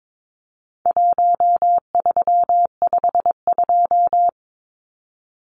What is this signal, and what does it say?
Morse code, heard as a keyed tone, about 22 wpm, 700 Hz. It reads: 1352
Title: Numbers - Contest Style - 4-Digit 22wpm